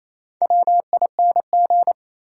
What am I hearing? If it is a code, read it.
Morse code, heard as a keyed tone, about 28 wpm, 700 Hz. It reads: WING
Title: Four Letter Words - 28wpm